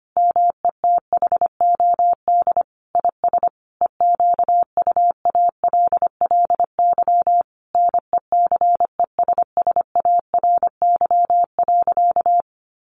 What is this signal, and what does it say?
Morse code, heard as a keyed tone, about 25 wpm, 700 Hz. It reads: METHOD IS EQUALLY NECESSARY.
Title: Sentences from Top 1400 Words 25wpm